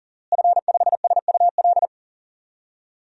Morse code, heard as a keyed tone, about 40 wpm, 700 Hz. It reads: F5SUL
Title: DXCC-100 Call Signs - 5 Characters - Repeated 3x - 40wpm